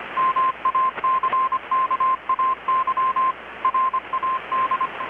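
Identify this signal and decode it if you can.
Morse code, heard as a keyed tone, about 25 wpm, 1050 Hz, with radio noise. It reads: MACKAY RAD
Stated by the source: Morse from KPH's Night of Nights XX maritime radiotelegraphy event, received in Chiapas, Mexico about 3,800 km away on a Kenwood R-600 with a grounded 10 m long-wire antenna.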